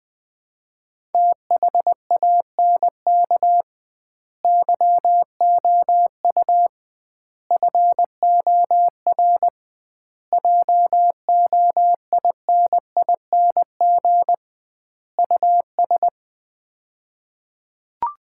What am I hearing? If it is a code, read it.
Morse code, heard as a keyed tone, about 20 wpm, 700 Hz. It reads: THANK YOU FOR JOINING US
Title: 2x Word-Spacing - Sets of 5 Words - 20wpm